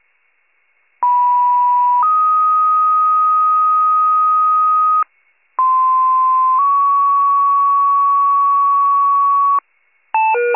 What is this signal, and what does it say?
Two-tone page, heard as a one-second tone, then a three-second tone, about 980 then 1250 Hz, about 1040 then 1120 Hz. It is heard from 1.0 to 9.6 s.